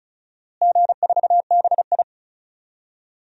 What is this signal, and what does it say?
Morse code, heard as a keyed tone, about 34 wpm, 700 Hz. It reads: G4BI
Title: DXCC-100 Call Signs - 4 Characters - Repeated 2x - 35wpm